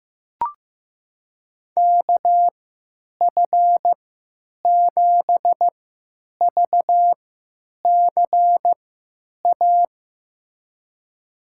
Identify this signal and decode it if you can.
Morse code, heard as a keyed tone, about 15 wpm, spaced as for 8 wpm, 700 Hz. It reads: KF7VCA
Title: USA Call Signs - 3x Character Spacing - 15wpm